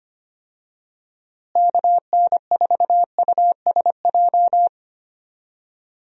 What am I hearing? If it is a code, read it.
Morse code, heard as a keyed tone, about 25 wpm, 700 Hz. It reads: KN4USJ
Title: USA Call Signs - 6 Characters - Repeated 3x - 25wpm